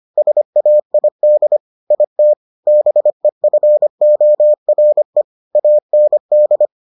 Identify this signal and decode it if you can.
Morse code, heard as a keyed tone, about 25 wpm, 600 Hz. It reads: SAID IT BEFORE AND